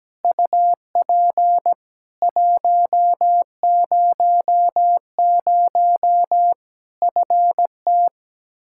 Morse code, heard as a keyed tone, about 17 wpm, 700 Hz. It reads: UP 100 FT
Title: K1USN SST QSO Practice 17wpm